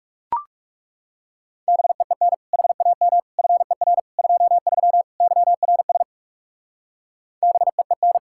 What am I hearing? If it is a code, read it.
Morse code, heard as a keyed tone, about 36 wpm, 700 Hz. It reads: BEEN HAM FER 23 YRS BEEN
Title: Speed-Racing Common QSO Phrases - 30wpm